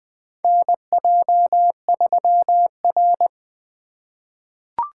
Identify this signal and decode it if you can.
Morse code, heard as a keyed tone, about 20 wpm, 700 Hz. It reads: NJ3R
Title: USA Call Signs - 3x Character Spacing - 20wpm